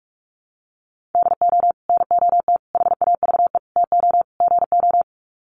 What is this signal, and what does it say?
Morse code, heard as a keyed tone, about 44 wpm, 700 Hz. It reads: DO NOT HAVE TO GO
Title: Sets of 5 Words 45wpm